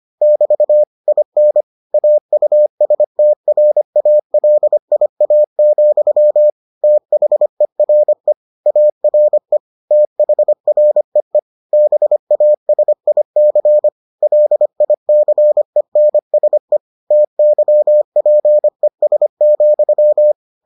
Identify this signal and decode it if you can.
Morse code, heard as a keyed tone, about 25 wpm, 600 Hz. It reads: <BT> IN AUSTRALIA, THERE ARE THREE BASIC LICENSE TYPES,